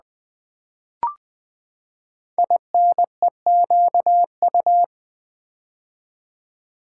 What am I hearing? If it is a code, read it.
Morse code, heard as a keyed tone, about 20 wpm, 700 Hz. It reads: INEQU
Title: Build A Word - 5 Letters At A Time - 10-Letter Words - 20wpm